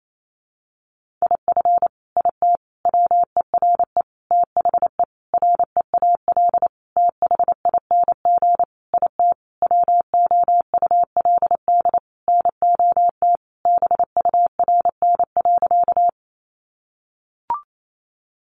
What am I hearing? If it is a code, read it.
Morse code, heard as a keyed tone, about 28 wpm, 700 Hz. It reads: IF IT WERE THE REAL THING IT WOULD NOT BURN.
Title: Sentences from Top 1300 Words 28wpm